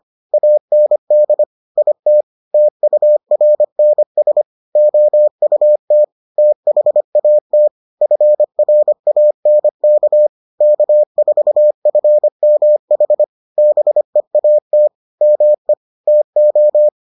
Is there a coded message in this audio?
Yes: Morse code from 0.3 to 16.9 s, heard as a keyed tone, about 25 wpm, 600 Hz. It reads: AND IT TURNS OUT THAT FRANK K4FMH BEAT ME TO